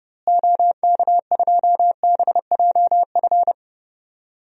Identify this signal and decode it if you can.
Morse code, heard as a keyed tone, about 30 wpm, 700 Hz. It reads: OK2BJF
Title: DXCC-100 Call Signs - 6 Characters - Repeated 3x - 30wpm